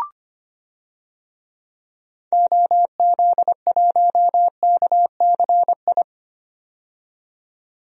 Morse code, heard as a keyed tone, about 25 wpm, 700 Hz. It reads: OZ1KCI
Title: DXCC-100 Call Signs - 6 Characters - 25wpm